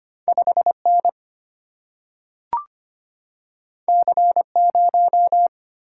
Morse code, heard as a keyed tone, about 25 wpm, 700 Hz. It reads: 5 N C 0
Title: Single-Letter Single-Number 25wpm